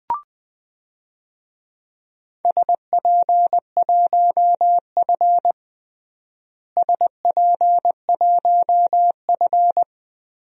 Morse code, heard as a keyed tone, about 20 wpm, 700 Hz. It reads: SP1F SP1F
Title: DXCC-100 Call Signs - 4 Characters - Repeated 2x - 20wpm